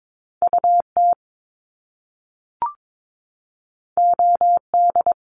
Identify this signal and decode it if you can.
Morse code, heard as a keyed tone, about 22 wpm, 700 Hz. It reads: UT OD
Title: Top Two Letter Combinations in Words 22wpm